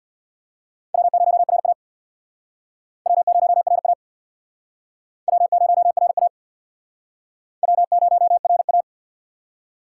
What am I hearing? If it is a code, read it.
Morse code, heard as a keyed tone, about 66 wpm, 700 Hz. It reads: W0RA W0RA W0RA W0RA